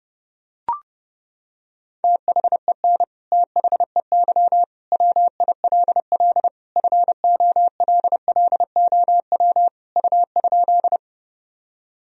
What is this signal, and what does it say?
Morse code, heard as a keyed tone, about 30 wpm, 700 Hz. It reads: THEN THEY WILL FOLLOW U?
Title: Sentences from Top 200 Words and Abbreviations 30wpm